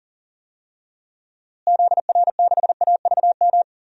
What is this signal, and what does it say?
Morse code, heard as a keyed tone, about 40 wpm, 700 Hz. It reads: ZR6AVM
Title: DXCC-100 Call Signs - 6 Characters - Repeated 3x - 40wpm